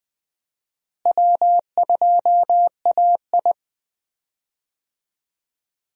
Morse code, heard as a keyed tone, about 20 wpm, 700 Hz. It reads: W2AI
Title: USA Call Signs - 4 Characters - Repeated 3x - 20wpm